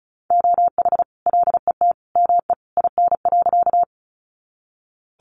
Morse code, heard as a keyed tone, about 34 wpm, 700 Hz. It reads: OH LET ME IN.
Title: Sentences from Top 500 Words 35wpm